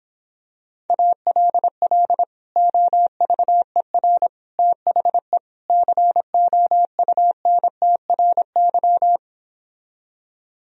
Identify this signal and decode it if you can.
Morse code, heard as a keyed tone, about 26 wpm, 700 Hz. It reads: ALL OVER THE COUNTRY